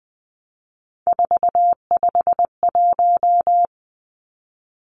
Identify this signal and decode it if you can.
Morse code, heard as a keyed tone, about 20 wpm, 700 Hz. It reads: 451
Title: Three Challenging Letter-Number Combinations 20wpm